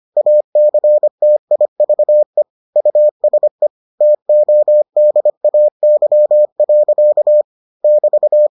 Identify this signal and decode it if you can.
Morse code, heard as a keyed tone, about 25 wpm, 600 Hz. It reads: ACTIVE USE TODAY. <BT>